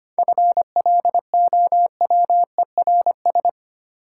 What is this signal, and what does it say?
Morse code, heard as a keyed tone, about 27 wpm, 700 Hz. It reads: FLOWERS
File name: Easy Words - Speed-Racing with Hints - 25wpm